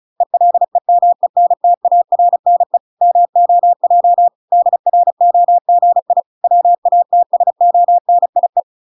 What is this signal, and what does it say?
Morse code, heard as a keyed tone, about 34 wpm, 700 Hz. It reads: ELEMENTARNE MOJ DROGI WATSONIE